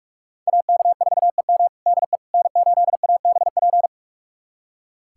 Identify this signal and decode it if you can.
Morse code, heard as a keyed tone, about 45 wpm, 700 Hz. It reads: AK4EM DE N8ABP
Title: USA Callsign DE Callsign - 5 Characters - 45wpm